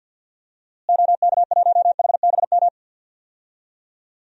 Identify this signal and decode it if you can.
Morse code, heard as a keyed tone, about 51 wpm, 700 Hz. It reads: OK1HBM
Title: DXCC-100 Call Signs - 6 Characters - Repeated 2x - 50wpm